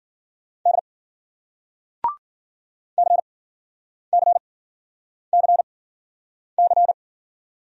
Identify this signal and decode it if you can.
Morse code, heard as a keyed tone, about 55 wpm, spaced as for 9 wpm, 700 Hz. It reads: D CCCC